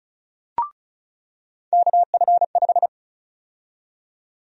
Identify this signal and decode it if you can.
Morse code, heard as a keyed tone, about 36 wpm, 700 Hz. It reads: KF5
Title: USA Call Sign Prefixes - 35wpm